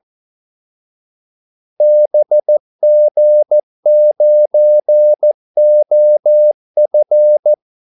Morse code, heard as a keyed tone, about 14 wpm, 600 Hz. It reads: BG9OF
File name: Callsigns-14wpm